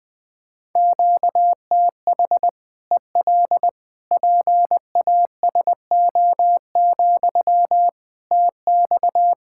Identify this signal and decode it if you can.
Morse code, heard as a keyed tone, about 20 wpm, 700 Hz. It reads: QTH EL PASO, TX
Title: Practice QSO - Exchange 1 - Beginning - 20wpm